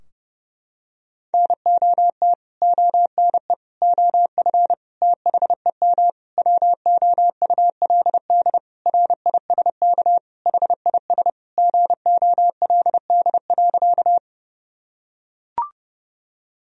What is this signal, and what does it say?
Morse code, heard as a keyed tone, about 30 wpm, 700 Hz. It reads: NOT ONE OF THEM WOULD RISK HIS GOLD.
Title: Sentences from Top 1000 Words - Review Part 4 - 30wpm